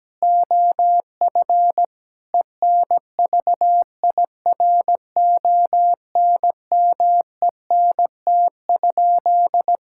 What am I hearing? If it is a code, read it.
Morse code, heard as a keyed tone, about 17 wpm, 700 Hz. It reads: OF ENVIRONMENT?
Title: Sentences from Top 1000 Words - Review Part 5 - 17wpm